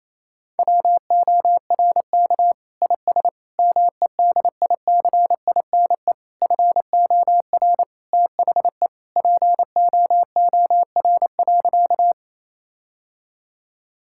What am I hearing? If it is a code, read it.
Morse code, heard as a keyed tone, about 28 wpm, 700 Hz. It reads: WORK IS MEDICINE FOR THE POOR.
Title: Sentences from Top 1000 Words - Review Part 6 - 28wpm